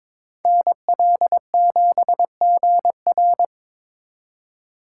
Morse code, heard as a keyed tone, about 26 wpm, 700 Hz. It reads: NL7GR
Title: Speed Racing - USA 2x2 Call Signs 22wpm